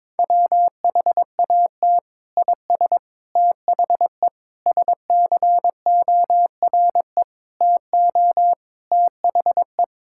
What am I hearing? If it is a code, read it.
Morse code, heard as a keyed tone, about 22 wpm, 700 Hz. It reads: WHAT IS THE SCORE TO THE
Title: Sentences from Top 1600 Words 22wpm